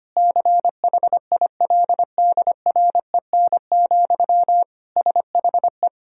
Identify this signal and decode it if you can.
Morse code, heard as a keyed tone, about 25 wpm, 700 Hz. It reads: CHILDREN, SHE
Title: Sentences from Top 1000 Words - Review Part 3 - 25wpm